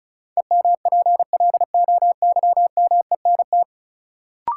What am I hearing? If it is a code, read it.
Morse code, heard as a keyed tone, about 36 wpm, 700 Hz. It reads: EMPLOYMENT
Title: Sentences from Top 1500 Words - New Word Drill - 35wpm